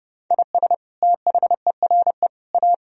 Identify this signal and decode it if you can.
Morse code, heard as a keyed tone, about 28 wpm, 700 Hz. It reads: IS THERE A
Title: Sentences from Top 1000 Words - Review Part 4 - 30wpm